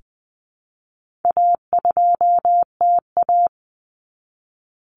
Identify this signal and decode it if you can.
Morse code, heard as a keyed tone, about 19 wpm, 700 Hz. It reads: A2TA
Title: Common Cut Numbers - 4-Digit 20wpm